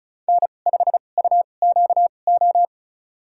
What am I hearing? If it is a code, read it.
Morse code, heard as a keyed tone, about 36 wpm, spaced as for 25 wpm, 700 Hz. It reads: N5UQO
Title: USA Call Signs - 2x Character Spacing - 35wpm